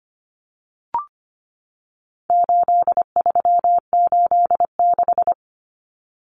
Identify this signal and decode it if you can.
Morse code, heard as a keyed tone, about 25 wpm, 700 Hz. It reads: 8386
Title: Numbers - 4-Digit 25wpm